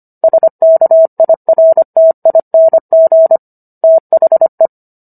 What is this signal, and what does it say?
Morse code, heard as a keyed tone, about 25 wpm, 650 Hz. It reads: SKIRTING THE